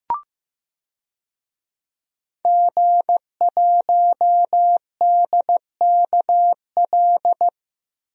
Morse code, heard as a keyed tone, about 15 wpm, 700 Hz. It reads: G1DKL
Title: DXCC-100 Call Signs - 5 Characters - 15wpm